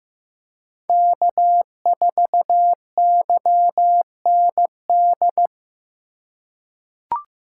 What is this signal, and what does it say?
Morse code, heard as a keyed tone, about 15 wpm, 700 Hz. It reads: K4YND
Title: K1USN SST QSO Practice - 15wpm